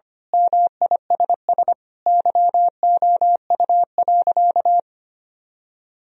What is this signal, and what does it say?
Morse code, heard as a keyed tone, about 25 wpm, 700 Hz. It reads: MISS YOU.
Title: Sentences from Top 1000 Words - Review Part 3 - 25wpm